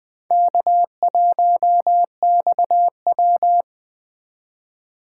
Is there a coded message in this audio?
Yes: Morse code from 0.3 to 3.6 s, heard as a keyed tone, about 20 wpm, 700 Hz. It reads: K1XW